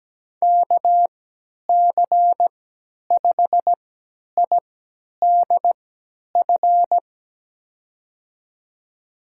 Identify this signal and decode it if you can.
Morse code, heard as a keyed tone, about 17 wpm, spaced as for 10 wpm, 700 Hz. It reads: KC5IDF
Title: USA Call Signs - 3x Character Spacing - 17wpm